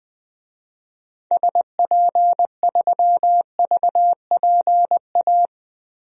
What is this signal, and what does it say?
Morse code, heard as a keyed tone, about 20 wpm, 700 Hz. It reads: SP3VPA
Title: DXCC-100 Call Signs - 6 Characters - 20wpm